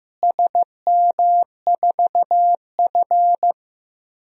Morse code, heard as a keyed tone, about 15 wpm, 700 Hz. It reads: SM4F
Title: DXCC-100 Call Signs - 4 Characters - Repeated 3x - 15wpm